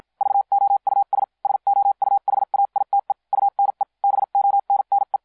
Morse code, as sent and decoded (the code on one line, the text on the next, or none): ...- --- ..- ... / ... --- ..- .... .- .. - . / ..- -. / -... --- -. -. .
VOUS SOUHAITE UNE BONNE